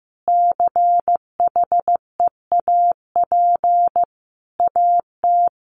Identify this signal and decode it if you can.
Morse code, heard as a keyed tone, about 15 wpm, 700 Hz. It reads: CHEAP AT
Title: Sentences from Top 1200 Words 15wpm